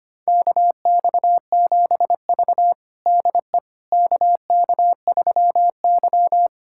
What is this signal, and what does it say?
Morse code, heard as a keyed tone, about 25 wpm, 700 Hz. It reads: KX7V DE KK3Y
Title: USA Callsign DE Callsign - 4 Characters - 25wpm